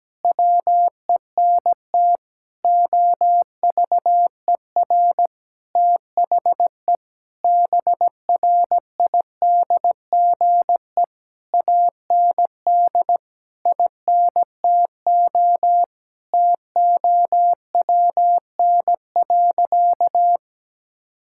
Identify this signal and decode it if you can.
Morse code, heard as a keyed tone, about 17 wpm, 700 Hz. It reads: WENT OVER THE BRIDGE AND INTO TOWN.
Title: Sentences from Top 1600 Words 17wpm